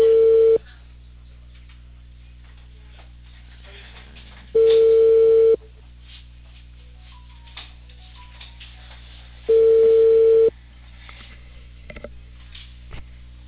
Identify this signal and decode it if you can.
Ringback tone (China, 450 Hz).